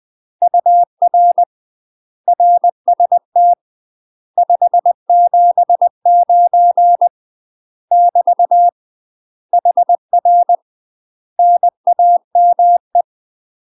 Morse code, heard as a keyed tone, about 20 wpm, 700 Hz. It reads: UR RST 579 <BT> HR NAME
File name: UK MASTER QSOs TRAINING / 20WPM / UK MASTER QSOs 0010